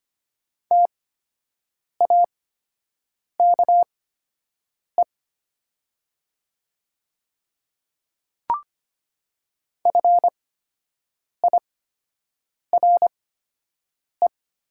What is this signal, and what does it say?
Morse code, heard as a keyed tone, about 25 wpm, spaced as for 7 wpm, 700 Hz. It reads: TAKE FIRE
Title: 8x Character Spacing - Top 500 Words - 25wpm